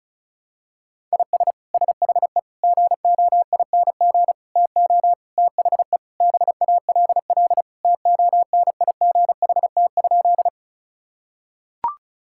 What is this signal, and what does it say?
Morse code, heard as a keyed tone, about 36 wpm, 700 Hz. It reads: IS SHE GOING TO THE BALL TONIGHT?
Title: Sentences from Top 1000 Words - Review Part 6 - 35wpm